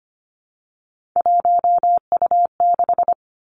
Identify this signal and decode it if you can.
Morse code, heard as a keyed tone, about 25 wpm, 700 Hz. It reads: 1U6
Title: Three Challenging Letter-Number Combinations 25wpm